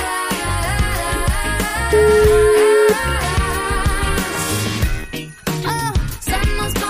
Ringback tone (Europe, 425 Hz), with noise.